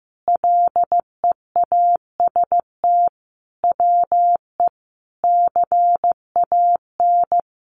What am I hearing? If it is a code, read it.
Morse code, heard as a keyed tone, about 15 wpm, 700 Hz. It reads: LEAST WE CAN
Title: Sentences from Top 700 Words 15wpm